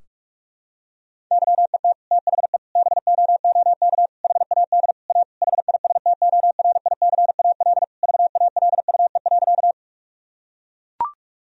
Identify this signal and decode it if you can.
Morse code, heard as a keyed tone, about 41 wpm, 700 Hz. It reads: YET THE BOOK HAD A HISTORICAL VALUE.
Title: Sentences from Top 1000 Words - Review Part 4 - 45wpm